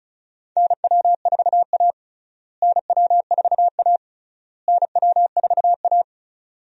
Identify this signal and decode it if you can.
Morse code, heard as a keyed tone, about 36 wpm, 700 Hz. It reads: NW4A NW4A NW4A
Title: USA Call Signs - 4 Characters - Repeated 3x - 35wpm